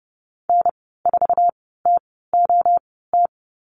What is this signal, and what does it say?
Morse code, heard as a keyed tone, about 30 wpm, spaced as for 17 wpm, 700 Hz. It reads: N4TOT
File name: Lesson 15g - Number 5 - Call Signs 30wpm